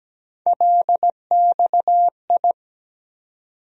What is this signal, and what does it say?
Morse code, heard as a keyed tone, about 17 wpm, 700 Hz. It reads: LXI
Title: USA Call Sign Postfixes - 3 Letters 17wpm